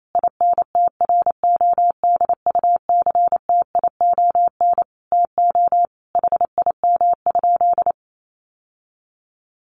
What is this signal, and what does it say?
Morse code, heard as a keyed tone, about 28 wpm, 700 Hz. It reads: INTRODUCTION TO HIM?
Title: Sentences from Top 1000 Words 28wpm